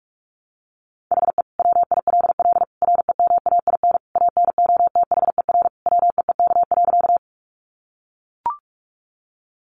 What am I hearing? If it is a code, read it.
Morse code, heard as a keyed tone, about 44 wpm, 700 Hz. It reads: HE WILL REMAIN ANOTHER WEEK.
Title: Sentences from Top 1200 Words 45wpm